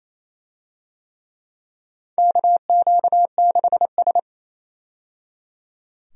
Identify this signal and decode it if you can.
Morse code, heard as a keyed tone, about 28 wpm, 700 Hz. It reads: KQ6S
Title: USA Call Signs - 4 Characters - 28wpm